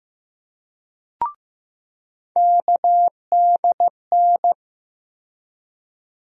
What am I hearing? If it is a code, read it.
Morse code, heard as a keyed tone, about 15 wpm, 700 Hz. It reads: KDN
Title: USA Call Sign Postfixes - 3 Letters 15wpm